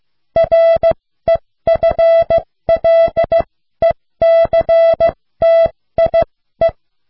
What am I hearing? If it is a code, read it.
Morse code, heard as a keyed tone, about 15 wpm, spaced as for 13 wpm, 660 Hz. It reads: REFLECTIE